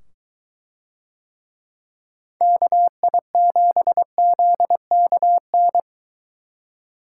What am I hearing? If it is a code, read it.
Morse code, heard as a keyed tone, about 22 wpm, 700 Hz. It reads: KI7ZKN